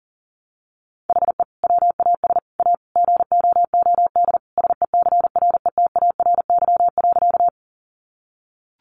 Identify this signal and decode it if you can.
Morse code, heard as a keyed tone, about 40 wpm, 700 Hz. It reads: HE WAS A GOOD SECRETARY.
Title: Sentences from Top 1300 Words 40wpm